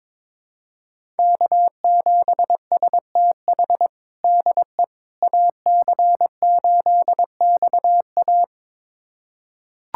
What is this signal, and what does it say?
Morse code, heard as a keyed tone, about 22 wpm, 700 Hz. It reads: K7STH DE AC8XA
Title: USA Callsign DE Callsign - 5 Characters - 22wpm